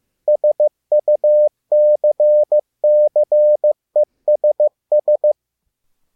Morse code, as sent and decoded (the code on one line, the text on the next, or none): ... ..- -.-. -.-. ... ...
SUCCESS